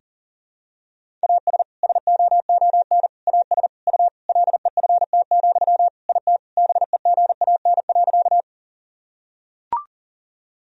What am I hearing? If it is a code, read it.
Morse code, heard as a keyed tone, about 40 wpm, 700 Hz. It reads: AS SOON AS U LEFT, IT BEGAN.
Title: Sentences from Top 400 Words and Abbreviations 40wpm